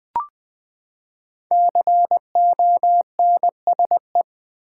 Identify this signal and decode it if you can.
Morse code, heard as a keyed tone, about 20 wpm, 700 Hz. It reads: CONSE
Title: Build A Word - 5 Letters At A Time - 12-Letter Words - 20wpm